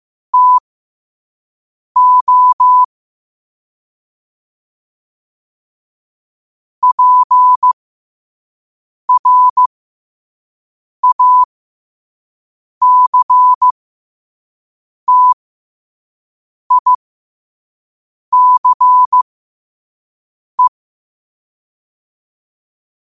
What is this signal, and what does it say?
Morse code, heard as a keyed tone, about 15 wpm, spaced as for 5 wpm, 1000 Hz. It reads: TO PRACTICE
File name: qotdfiles / 5wpm